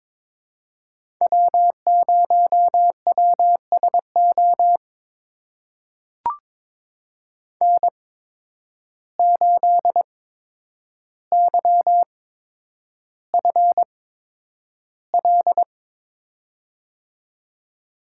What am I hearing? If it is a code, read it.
Morse code, heard as a keyed tone, about 22 wpm, spaced as for 9 wpm, 700 Hz. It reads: W0WSO N8YFL
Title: USA Call Signs - 5 Characters - 8x Character Spacing - 22wpm